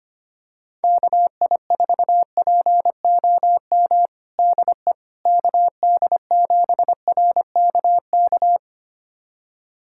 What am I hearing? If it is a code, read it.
Morse code, heard as a keyed tone, about 25 wpm, 700 Hz. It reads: KI4POM DE KD7RKK
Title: USA Callsign DE Callsign - 6 Characters - 25wpm